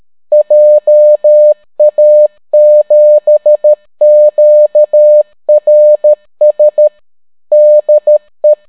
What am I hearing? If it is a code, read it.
Morse code, heard as a keyed tone, about 13 wpm, 600 Hz. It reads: JA7QRS DE